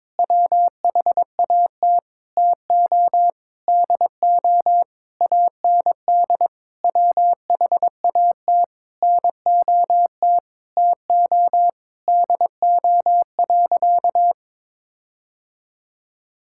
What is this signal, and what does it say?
Morse code, heard as a keyed tone, about 22 wpm, 700 Hz. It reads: WHAT TO DO AND WHAT NOT TO DO.
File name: practice / 22wpm / Sentences with 8-Word Phrases - 22wpm